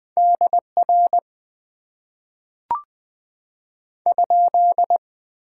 Morse code, heard as a keyed tone, about 20 wpm, 700 Hz. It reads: DR ?
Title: Top 90 QSO Elements - 8x Character Spacing - 20wpm